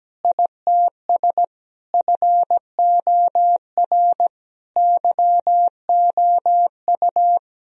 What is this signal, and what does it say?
Morse code, heard as a keyed tone, about 17 wpm, 700 Hz. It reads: ITS FOR YOU
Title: Sentences from Top 1000 Words - Review Part 3 - 17wpm